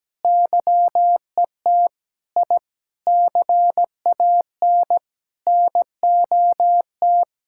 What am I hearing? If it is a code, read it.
Morse code, heard as a keyed tone, about 17 wpm, 700 Hz. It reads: YET I CAN NOT
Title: Sentences from Top 1000 Words - Review Part 6 - 17wpm